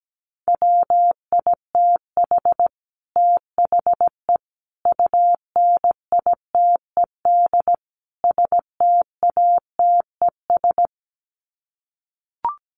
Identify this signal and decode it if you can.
Morse code, heard as a keyed tone, about 17 wpm, 700 Hz. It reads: WITH THE UNITED STATES
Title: Sets of 4 Words 17wpm